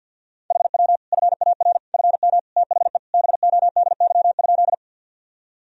Morse code, heard as a keyed tone, about 51 wpm, 700 Hz. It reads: HW FAR FM THE BODY?
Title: Sentences from Top 400 Words and Abbreviations 50wpm